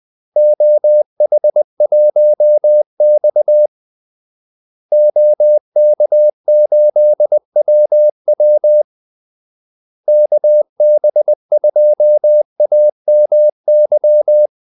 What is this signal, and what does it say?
Morse code, heard as a keyed tone, about 20 wpm, 600 Hz. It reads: OH1X OK8WW KB2AMY
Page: Kaldesignaler 20 wpm